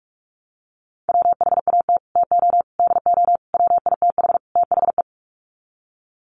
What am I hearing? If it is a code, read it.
Morse code, heard as a keyed tone, about 44 wpm, 700 Hz. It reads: WHAT TO DO WITH THE
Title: Sets of 5 Words Encore2 45wpm